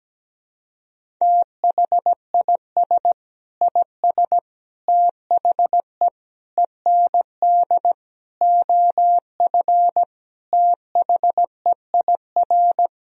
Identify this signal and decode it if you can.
Morse code, heard as a keyed tone, about 17 wpm, 700 Hz. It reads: THIS IS THE END OF THEIR
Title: Sentences from Top 1000 Words - Review Part 1 - 17wpm